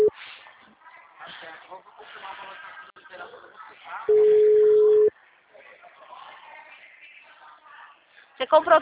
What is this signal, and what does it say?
Ringback tone (Europe, 425 Hz), with speech.